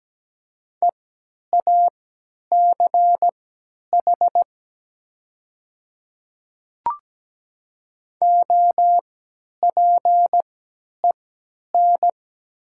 Morse code, heard as a keyed tone, about 17 wpm, spaced as for 10 wpm, 700 Hz. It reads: EACH OPEN
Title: 3x Character Spacing - Top 300 Words - 17wpm